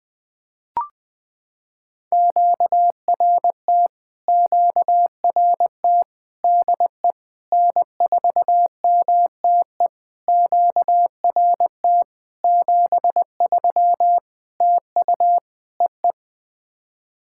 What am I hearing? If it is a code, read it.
Morse code, heard as a keyed tone, about 20 wpm, 700 Hz. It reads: QRT QRT DE N4MTE QRT 73 TU EE
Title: POTA QSO Practice - 20wpm